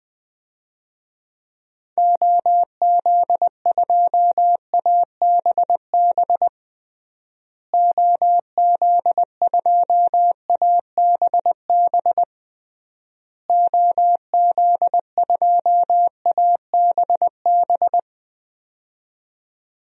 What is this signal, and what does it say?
Morse code, heard as a keyed tone, about 21 wpm, 700 Hz. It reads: OZ2ABB OZ2ABB OZ2ABB
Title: DXCC-100 Call Signs - 6 Characters - Repeated 3x - 20wpm